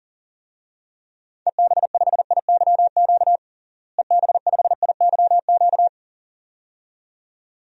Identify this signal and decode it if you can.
Morse code, heard as a keyed tone, about 37 wpm, 700 Hz. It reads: EB5IYQ EB5IYQ
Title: DXCC-100 Call Signs - 6 Characters - Repeated 2x - 40wpm